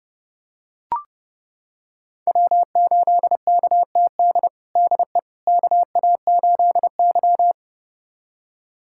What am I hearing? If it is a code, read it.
Morse code, heard as a keyed tone, about 30 wpm, 700 Hz. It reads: W8KTD DE KA8Y
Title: USA Callsign DE Callsign 30wpm